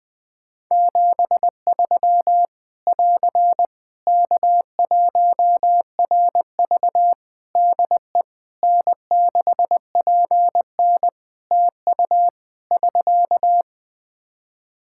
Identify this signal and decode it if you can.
Morse code, heard as a keyed tone, about 20 wpm, 700 Hz. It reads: 73 <AR> K1RV DE N6PN TU <SK>
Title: Practice QSO - Exchange 4 - Ending - 20wpm